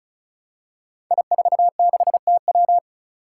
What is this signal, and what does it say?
Morse code, heard as a keyed tone, about 35 wpm, 700 Hz. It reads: I46TW
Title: Five Challenging Letter-Number Combinations 35wpm